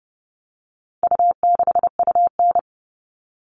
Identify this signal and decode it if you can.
Morse code, heard as a keyed tone, about 30 wpm, 700 Hz. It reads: U6UN
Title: All Cut Numbers - 4-Digit 30wpm